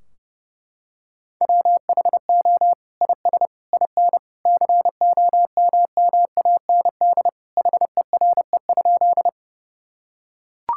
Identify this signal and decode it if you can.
Morse code, heard as a keyed tone, about 30 wpm, 700 Hz. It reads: WHO IS IN COMMAND HERE?
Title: Sentences from Top 1600 Words 30wpm